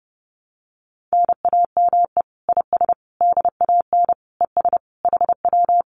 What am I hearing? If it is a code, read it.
Morse code, heard as a keyed tone, about 30 wpm, 700 Hz. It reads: NAME IS DAN ES HW
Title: Common QSO Phrases Encore 30wpm